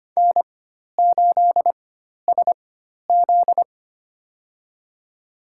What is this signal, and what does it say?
Morse code, heard as a keyed tone, about 25 wpm, spaced as for 12 wpm, 700 Hz. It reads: N8SZ
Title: USA Call Signs - 4 Characters - 4x Character Spacing - 25wpm